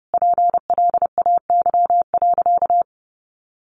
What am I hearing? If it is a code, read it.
Morse code, heard as a keyed tone, about 30 wpm, 700 Hz. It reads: PLAY.
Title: Sentences from Top 1300 Words 30wpm